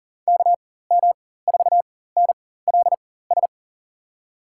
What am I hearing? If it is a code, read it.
Morse code, heard as a keyed tone, about 40 wpm, spaced as for 19 wpm, 700 Hz. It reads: KM4NLS
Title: USA Call Signs - 6 Characters - 4x Character Spacing - 40wpm